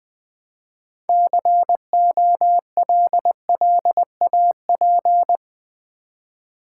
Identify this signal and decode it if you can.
Morse code, heard as a keyed tone, about 20 wpm, 690 Hz. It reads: COLLAP
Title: Build A Word - 3 Letters At A Time - 8-Letter Words - 20wpm